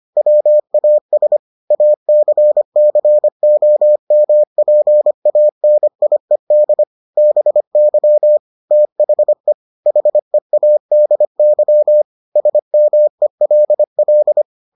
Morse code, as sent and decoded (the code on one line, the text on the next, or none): .-- .- ... / .- -.-. -.-. --- -- .--. .- -. .. -.. / -... -.-- / - .... / .... .- -.. -.-- / ... -- . .-.. .-..
WAS ACCOMPANIED BY THE HEADY SMELL